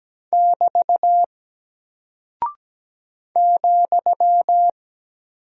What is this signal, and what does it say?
Morse code, heard as a keyed tone, about 17 wpm, 700 Hz. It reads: <BT> ,
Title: Single Character (Letter:Number:Punctuation:Prosign) Alt 17wpm